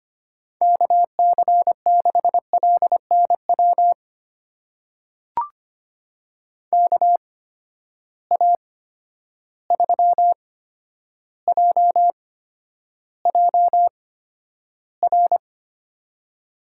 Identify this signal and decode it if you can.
Morse code, heard as a keyed tone, about 25 wpm, spaced as for 11 wpm, 700 Hz. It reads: KC6LNW KA3JJR
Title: USA Call Signs - 6 Characters - 8x Character Spacing - 25wpm